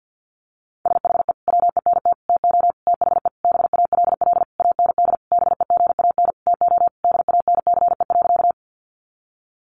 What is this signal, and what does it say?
Morse code, heard as a keyed tone, about 50 wpm, 700 Hz. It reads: SHE WENT TO THE BALL AND BEGAN TO DANCE.
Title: Sentences from Top 900 Words 50wpm